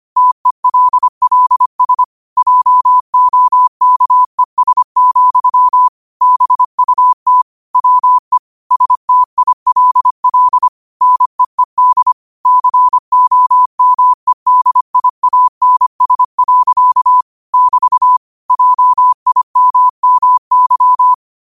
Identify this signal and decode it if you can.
Morse code, heard as a keyed tone, about 25 wpm, 1000 Hz. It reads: TELLS JOKES, BUT WE STILL NEED COMEDIANS. <BT> JIMMY
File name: qotdfiles / 25wpm